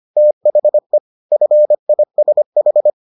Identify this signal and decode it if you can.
Morse code, heard as a keyed tone, about 25 wpm, 600 Hz. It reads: THE FISH